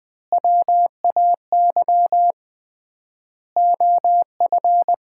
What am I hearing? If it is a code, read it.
Morse code, heard as a keyed tone, about 21 wpm, 700 Hz. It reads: WAY OF
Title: 3x Word-Spacing - Sets of 4 Words - 20wpm